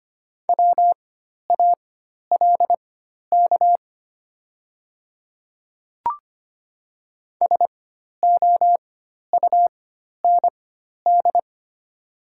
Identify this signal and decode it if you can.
Morse code, heard as a keyed tone, about 25 wpm, spaced as for 12 wpm, 690 Hz. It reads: WALK SOUND
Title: 4x Character Spacing - Top 300 Words - 25wpm